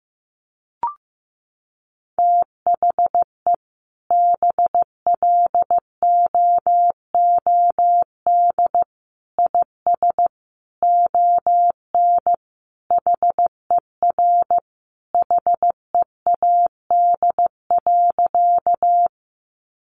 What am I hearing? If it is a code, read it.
Morse code, heard as a keyed tone, about 15 wpm, 700 Hz. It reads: THE BLOOD IS ON HER HEAD.
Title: Sentences from Top 1100 Words 15wpm